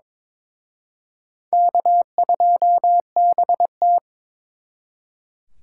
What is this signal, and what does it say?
Morse code, heard as a keyed tone, about 26 wpm, 700 Hz. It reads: K2BT
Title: Speed Racing - USA 1x2 Call Signs 22wpm